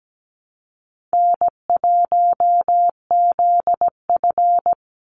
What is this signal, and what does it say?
Morse code, heard as a keyed tone, about 17 wpm, 700 Hz. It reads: N1ZF
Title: NA SOTA Chaser Callsigns 17wpm